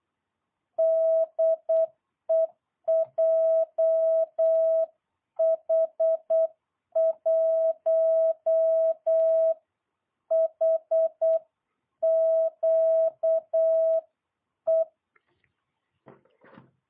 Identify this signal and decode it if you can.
Morse code, heard as a keyed tone, about 8 wpm, 650 Hz. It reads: DE JH1HQE